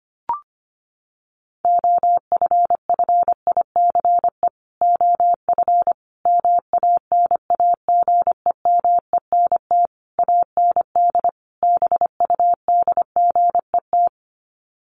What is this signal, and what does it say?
Morse code, heard as a keyed tone, about 25 wpm, 700 Hz. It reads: OFFICE OF MANAGEMENT AND BUDGET
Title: Sets of 5 Words Encore2 25wpm